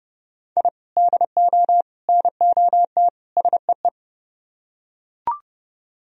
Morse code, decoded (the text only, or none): I DO NOT SEE